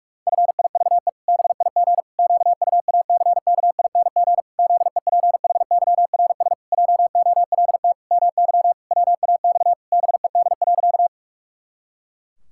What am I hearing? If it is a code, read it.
Morse code, heard as a keyed tone, about 46 wpm, 700 Hz. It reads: FIVE BIG QUACKING ZEPHYRS JOLT MY WAX BED.
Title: Pangram Challenge v2 - 45wpm